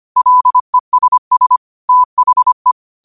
Morse code, heard as a keyed tone, about 25 wpm, 1000 Hz. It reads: LESS THE